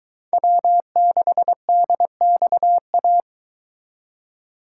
Morse code, heard as a keyed tone, about 23 wpm, 700 Hz. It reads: W6DXA